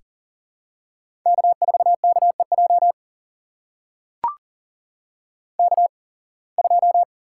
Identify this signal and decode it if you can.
Morse code, heard as a keyed tone, about 40 wpm, spaced as for 26 wpm, 700 Hz. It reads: K4KEJ K2
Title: USA Call Signs - 5 Characters - 8x Character Spacing - 40wpm